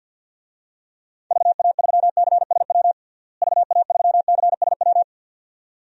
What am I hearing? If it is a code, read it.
Morse code, heard as a keyed tone, about 49 wpm, 700 Hz. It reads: VA3CSW VA3CSW
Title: DXCC-100 Call Signs - 6 Characters - Repeated 2x - 50wpm